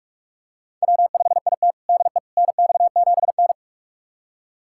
Morse code, heard as a keyed tone, about 41 wpm, 700 Hz. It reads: W5IT DE NX7N
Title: USA Callsign DE Callsign - 4 Characters - 45wpm